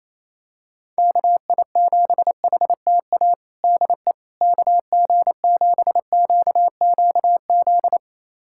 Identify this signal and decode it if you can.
Morse code, heard as a keyed tone, about 27 wpm, 700 Hz. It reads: KI7HTA DE KG7QQZ
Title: USA Callsign DE Callsign - 6 Characters - 28wpm